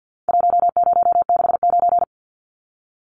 Morse code, heard as a keyed tone, about 48 wpm, 700 Hz. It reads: 1069
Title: Numbers - Contest Style - 4-Digit 50wpm